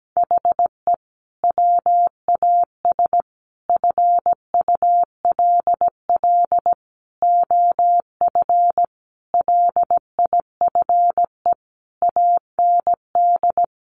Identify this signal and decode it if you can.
Morse code, heard as a keyed tone, about 17 wpm, 700 Hz. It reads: HE WAS FULL OF LIFE AND
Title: Sentences from Top 1100 Words 17wpm